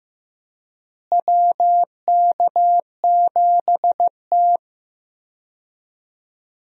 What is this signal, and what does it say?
Morse code, heard as a keyed tone, about 15 wpm, 700 Hz. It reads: WK7T
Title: USA Call Signs - 4 Characters - Repeated 2x - 15wpm